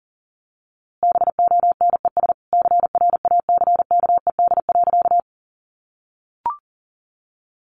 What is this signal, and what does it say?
Morse code, heard as a keyed tone, about 40 wpm, 700 Hz. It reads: BONES CRACKED.
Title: A Sound Of Thunder 40wpm